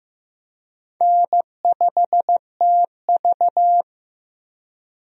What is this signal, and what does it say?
Morse code, heard as a keyed tone, about 16 wpm, 700 Hz. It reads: N5TV
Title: USA Call Signs - 4 Characters - Repeated 2x - 15wpm